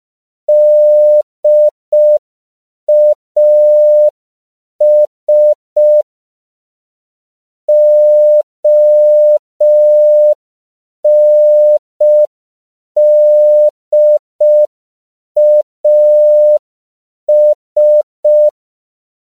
Morse code, heard as a keyed tone, about 5 wpm, 600 Hz. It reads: DAS ONDAS
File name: CW / 5PPM